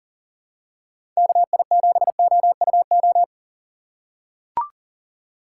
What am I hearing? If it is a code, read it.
Morse code, heard as a keyed tone, about 40 wpm, 700 Hz. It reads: KI7OUO
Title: USA Call Signs - 6 Characters - 3x Character Spacing - 40wpm